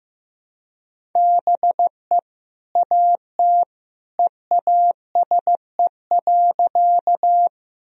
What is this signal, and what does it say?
Morse code, heard as a keyed tone, about 15 wpm, 700 Hz. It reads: BE AT EASE.
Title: Sentences from Top 1000 Words - Review Part 1 - 15wpm